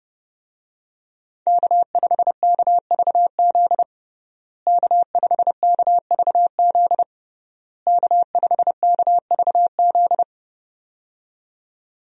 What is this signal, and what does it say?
Morse code, heard as a keyed tone, about 32 wpm, 700 Hz. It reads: K5KVZ K5KVZ K5KVZ
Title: USA Call Signs - 5 Characters - Repeated 3x - 30wpm